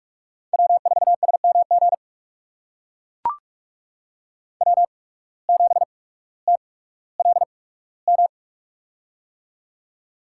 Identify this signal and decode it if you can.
Morse code, heard as a keyed tone, about 44 wpm, spaced as for 19 wpm, 700 Hz. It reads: W4SMG W7TLM
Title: USA Call Signs - 5 Characters - 8x Character Spacing - 45wpm